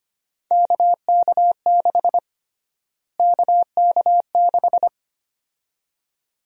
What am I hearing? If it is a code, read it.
Morse code, heard as a keyed tone, about 25 wpm, 700 Hz. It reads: KK6 KK6
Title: USA Call Sign Prefixes - Repeated 2x - 25wpm